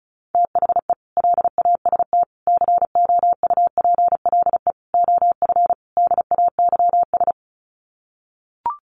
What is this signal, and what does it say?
Morse code, heard as a keyed tone, about 34 wpm, 700 Hz. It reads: THE LAST COUPLE OF DAYS
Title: Sets of 5 Words Encore 35wpm